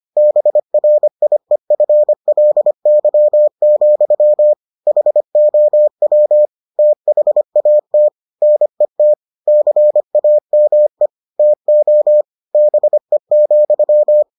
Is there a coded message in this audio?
Yes: Morse code from 0.2 to 14.2 s, heard as a keyed tone, about 25 wpm, 600 Hz. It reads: BRIEFLY, HOW THAT NET CAME TO BE,